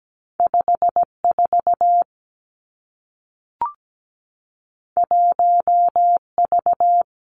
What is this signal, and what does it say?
Morse code, heard as a keyed tone, about 17 wpm, 700 Hz. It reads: 54 1V
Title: Two Challenging Letter-Number Combinations 17wpm